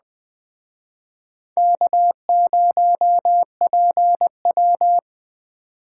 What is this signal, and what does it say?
Morse code, heard as a keyed tone, about 20 wpm, 700 Hz. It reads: K0PW
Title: Speed Racing - USA 1x2 Call Signs 17wpm